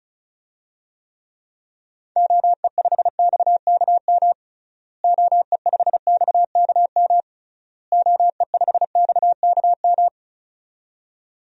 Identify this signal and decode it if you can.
Morse code, heard as a keyed tone, about 36 wpm, 700 Hz. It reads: OE5XKM OE5XKM OE5XKM
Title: DXCC-100 Call Signs - 6 Characters - Repeated 3x - 35wpm